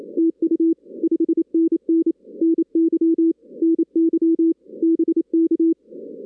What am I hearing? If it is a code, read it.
Morse code, heard as a keyed tone, about 29 wpm, 330 Hz. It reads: TU 5NN NY NY BK
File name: 2016 arrl dx cw / 2016-ARRL-DX-CW-K5TR- - 0-41